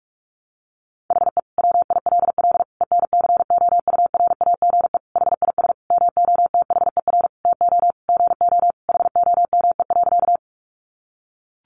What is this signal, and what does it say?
Morse code, heard as a keyed tone, about 44 wpm, 700 Hz. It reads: HE WILL ENCOURAGE HIS MOTHER TO GO HOME.
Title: Sentences from Top 1400 Words 45wpm